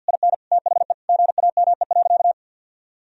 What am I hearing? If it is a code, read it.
Morse code, heard as a keyed tone, about 51 wpm, 700 Hz. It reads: IN THE GAME.
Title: Sentences from Top 1000 Words - Review Part 3 - 50wpm